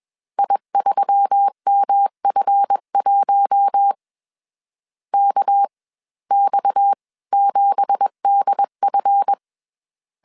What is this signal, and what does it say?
Morse code, heard as a keyed tone, about 21 wpm, spaced as for 17 wpm, 800 Hz. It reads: I3MF1 KX7DF
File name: CW / DET / 1 9 1500C S C 20wpm 2310